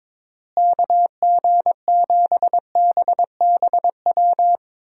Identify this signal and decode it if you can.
Morse code, heard as a keyed tone, about 23 wpm, 700 Hz. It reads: KG7BBW
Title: USA Call Signs - 6 Characters - Repeated 3x - 22wpm